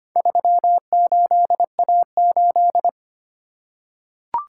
Morse code, heard as a keyed tone, about 25 wpm, 700 Hz. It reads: 38A8
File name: practice / 25wpm / Four Letter-Number Combinations - 25wpm